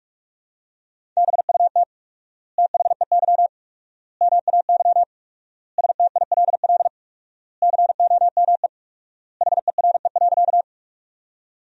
Morse code, heard as a keyed tone, about 41 wpm, 700 Hz. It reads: BUT THEY MAY STILL COME HERE.
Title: 4x Word-Spacing - Sentences from Top 300 Words - 45wpm